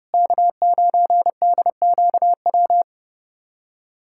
Morse code, heard as a keyed tone, about 32 wpm, 700 Hz. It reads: K9DQW
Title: USA Callsign DE Callsign 30wpm